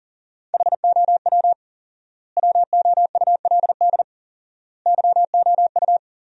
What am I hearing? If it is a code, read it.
Morse code, heard as a keyed tone, about 40 wpm, 700 Hz. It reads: HOW WOULD YOU
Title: 4x Word-Spacing - Sentences from Top 100 Words - 40wpm